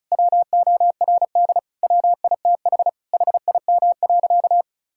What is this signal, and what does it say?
Morse code, heard as a keyed tone, about 36 wpm, 700 Hz. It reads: WORD WITH HIM.
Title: Sentences from Top 1000 Words - Review Part 6 - 35wpm